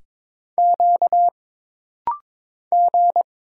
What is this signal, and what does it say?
Morse code, heard as a keyed tone, about 22 wpm, 700 Hz. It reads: Q G
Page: This rapid-fire timing is geared to push your ICR (Instant Character Recognition) skills.